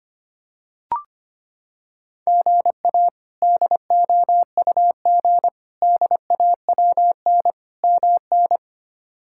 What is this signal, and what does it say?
Morse code, heard as a keyed tone, about 25 wpm, 700 Hz. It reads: GA DOUG DAWN MN
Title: K1USN SST QSO Practice - 25wpm